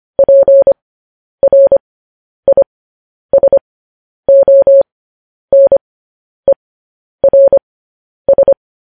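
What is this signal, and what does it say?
Morse code, heard as a keyed tone, about 25 wpm, spaced as for 10 wpm, 550 Hz. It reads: PRISONERS
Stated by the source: Morse code news headlines sent at 10 wpm.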